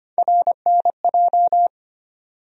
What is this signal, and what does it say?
Morse code, heard as a keyed tone, about 25 wpm, 700 Hz. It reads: RNJ
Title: USA Call Sign Postfixes - 3 Letters 25wpm